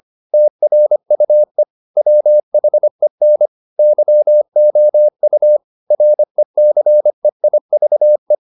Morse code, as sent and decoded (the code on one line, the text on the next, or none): - .-. ..- . / .-- .... -. / -.-- --- ..- / .-. -.-. .. ...- .
TRUE WHEN YOU RECEIVE